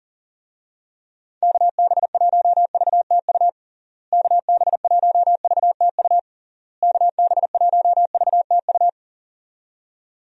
Morse code, heard as a keyed tone, about 40 wpm, 700 Hz. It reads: KB1VTU KB1VTU KB1VTU
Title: USA Call Signs - 6 Characters - Repeated 3x - 40wpm